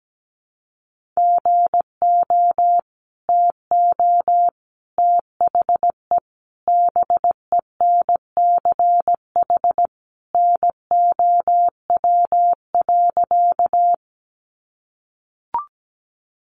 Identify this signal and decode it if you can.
Morse code, heard as a keyed tone, about 17 wpm, 700 Hz. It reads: GO TO THE BENCH NOW.
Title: Sentences from Top 1400 Words 17wpm